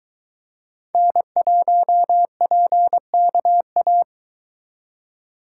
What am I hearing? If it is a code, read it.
Morse code, heard as a keyed tone, about 22 wpm, 700 Hz. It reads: N1PKA